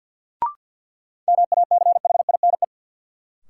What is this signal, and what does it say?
Morse code, heard as a keyed tone, about 52 wpm, 700 Hz. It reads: MACHINE
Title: Sentences from Top 1000 Words - Review Part 2 - New Word Drill - 50wpm